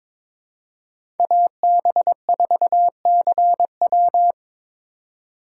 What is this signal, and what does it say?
Morse code, heard as a keyed tone, about 22 wpm, 700 Hz. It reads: AB4CW
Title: USA Call Signs - 5 Characters - 4x Character Spacing - 22wpm